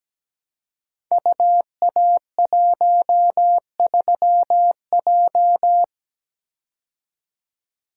Morse code, heard as a keyed tone, about 17 wpm, 700 Hz. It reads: UA13J
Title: Code Groups - 5 Letter:Numbers - 17wpm